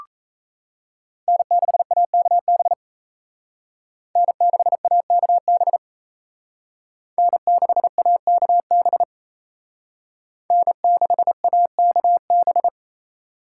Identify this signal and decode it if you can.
Morse code, heard as a keyed tone, about 36 wpm, 700 Hz. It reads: N6AKB N6AKB N6AKB N6AKB